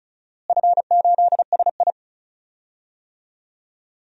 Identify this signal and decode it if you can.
Morse code, heard as a keyed tone, about 34 wpm, 700 Hz. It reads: F8SI
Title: DXCC-100 Call Signs - 4 Characters - Repeated 2x - 35wpm